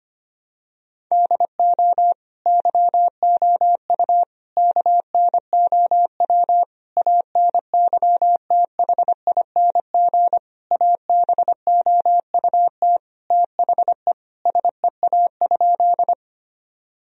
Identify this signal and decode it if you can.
Morse code, heard as a keyed tone, about 25 wpm, 700 Hz. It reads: DO YOU KNOW ANYTHING ABOUT THE SEA?
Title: Sentences from Top 1000 Words - Review Part 1 - 25wpm